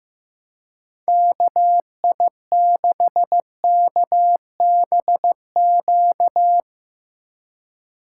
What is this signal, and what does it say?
Morse code, heard as a keyed tone, about 15 wpm, 700 Hz. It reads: KI6KBQ
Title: Speed Racing - USA 2x3 Call Signs 15wpm